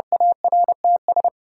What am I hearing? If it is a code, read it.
Morse code, heard as a keyed tone, about 30 wpm, 700 Hz. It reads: ARTS
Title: Sentences from Top 1000 Words - Review Part 6 - New Word Drill - 30wpm